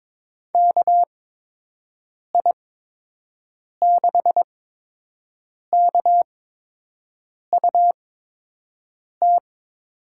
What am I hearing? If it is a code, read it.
Morse code, heard as a keyed tone, about 22 wpm, spaced as for 6 wpm, 700 Hz. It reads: KI6KUT
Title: USA Call Signs - 6 Characters - 8x Character Spacing - 22wpm